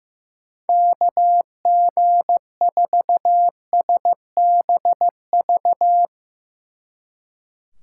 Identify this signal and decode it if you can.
Morse code, heard as a keyed tone, about 15 wpm, 700 Hz. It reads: KG4SBV
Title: Speed Racing - USA 2x3 Call Signs 15wpm